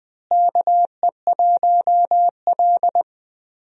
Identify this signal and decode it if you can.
Morse code, heard as a keyed tone, about 20 wpm, 700 Hz. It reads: KE1L
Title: USA Call Signs - 4 Characters - Repeated 2x - 20wpm